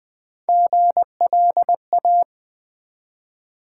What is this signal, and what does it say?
Morse code, heard as a keyed tone, about 20 wpm, 700 Hz. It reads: GLA
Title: Build A Word - 1 Letter At A Time - 3 to 5-Letter Words - 20wpm